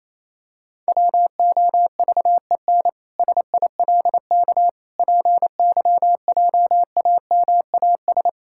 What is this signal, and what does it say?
Morse code, heard as a keyed tone, about 28 wpm, 700 Hz. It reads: WOVEN SILK PYJAMAS
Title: Pangram Challenge v2 - 28wpm